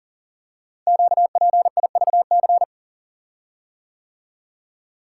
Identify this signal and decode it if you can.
Morse code, heard as a keyed tone, about 40 wpm, 700 Hz. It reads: QPIVC
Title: Code Groups - 5 Letters - 40wpm